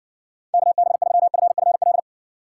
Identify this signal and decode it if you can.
Morse code, heard as a keyed tone, about 60 wpm, 700 Hz. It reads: KB3FFL